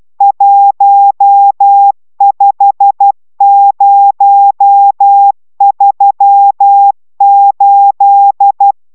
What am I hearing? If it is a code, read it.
Morse code, heard as a keyed tone, about 12 wpm, 800 Hz. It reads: 15038